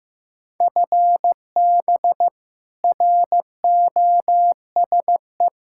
Morse code, heard as a keyed tone, about 15 wpm, 700 Hz. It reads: FB ROSE
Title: Practice QSO - Exchange 3 - More Info - 15wpm